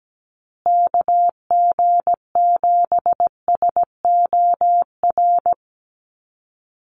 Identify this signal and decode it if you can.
Morse code, heard as a keyed tone, about 17 wpm, 700 Hz. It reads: KG7SOR
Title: USA Call Signs Repeated 2x 17wpm